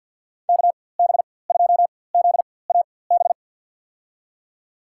Morse code, heard as a keyed tone, about 48 wpm, spaced as for 23 wpm, 700 Hz. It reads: KB2ZAB
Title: USA Call Signs - 6 Characters - 4x Character Spacing - 50wpm